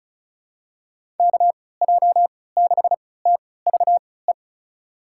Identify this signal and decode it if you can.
Morse code, heard as a keyed tone, about 36 wpm, spaced as for 20 wpm, 700 Hz. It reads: KJ6TVE
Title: USA Call Signs - 6 Characters - 3x Character Spacing - 35wpm